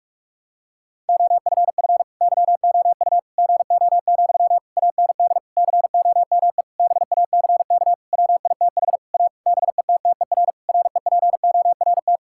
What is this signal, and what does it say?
Morse code, heard as a keyed tone, about 46 wpm, 700 Hz. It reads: OFF YOU GO, AND COME BACK WITH A BETTER REPORT
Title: Sentences from Top 1000 Words - Review Part 6 - 45wpm